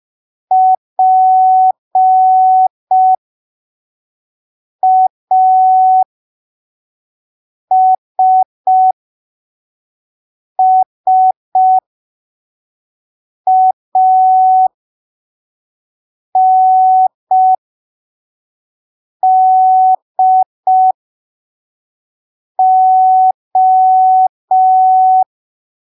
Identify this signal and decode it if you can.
Morse code, heard as a keyed tone, about 5 wpm, spaced as for 3 wpm, 750 Hz. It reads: PASSANDO
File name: arquivos / texto / texto 26 05ppm